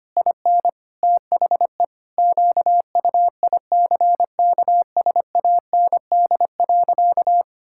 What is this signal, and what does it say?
Morse code, heard as a keyed tone, about 25 wpm, 700 Hz. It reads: IN THE QUICKSAND.
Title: Pangram Challenge v2 - 25wpm